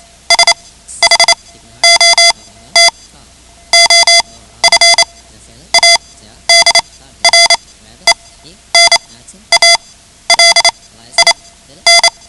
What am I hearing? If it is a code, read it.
Morse code, heard as a keyed tone, about 28 wpm, spaced as for 13 wpm, 2050 Hz. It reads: SHOT OF ADRENALIN
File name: ham radio / CodeQuick / Code Quick 10